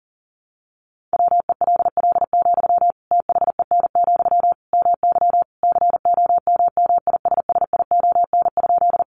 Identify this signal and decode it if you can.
Morse code, heard as a keyed tone, about 40 wpm, 700 Hz. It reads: WELL, THEN, MY COMMISSION?
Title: Sentences from Top 1400 Words 40wpm